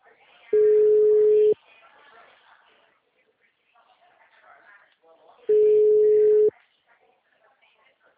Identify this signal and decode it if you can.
Ringback tone (Europe, 425 Hz), with speech.
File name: out- -2012- - - .53